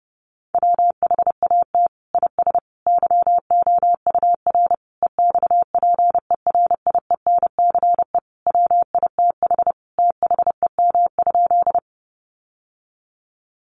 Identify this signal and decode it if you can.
Morse code, heard as a keyed tone, about 30 wpm, 700 Hz. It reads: WHAT IS YOUR EXPERIENCE WITH THEM?
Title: Sentences from Top 600 Words 30wpm